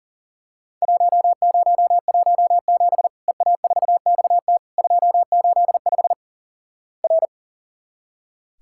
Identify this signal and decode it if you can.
Morse code, heard as a keyed tone, about 44 wpm, 700 Hz. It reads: 1017 EA4XT 285 R
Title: Worked All Europe DX Contest - QTC Exchange - 40wpm